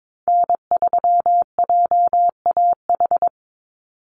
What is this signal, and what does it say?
Morse code, heard as a keyed tone, about 22 wpm, 700 Hz. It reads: N3JAH
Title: USA Call Signs 22wpm